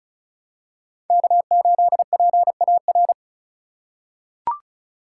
Morse code, heard as a keyed tone, about 36 wpm, 700 Hz. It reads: K8PAR
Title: USA Call Signs - 5 Characters - 2x Character Spacing - 35wpm